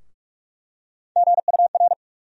Morse code, heard as a keyed tone, about 41 wpm, 700 Hz. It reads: GUR
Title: USA Call Sign Postfixes - 3 Letters 45wpm